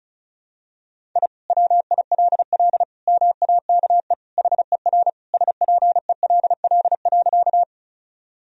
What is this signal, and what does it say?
Morse code, heard as a keyed tone, about 34 wpm, 700 Hz. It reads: I WILL MAKE HER SPELL.
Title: Sentences from Top 1000 Words - Review Part 1 - 35wpm